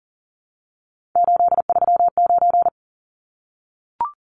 Morse code, heard as a keyed tone, about 40 wpm, 700 Hz. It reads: 839
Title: Numbers - 3-Digit 40wpm